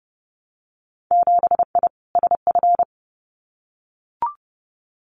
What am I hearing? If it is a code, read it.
Morse code, heard as a keyed tone, about 30 wpm, 700 Hz. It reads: 7I SF
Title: Winter Field Day Exchanges Encore 30wpm